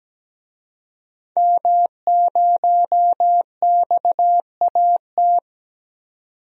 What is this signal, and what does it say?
Morse code, heard as a keyed tone, about 17 wpm, 700 Hz. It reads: M0XAT
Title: DXCC-100 Call Signs - 5 Characters - 17wpm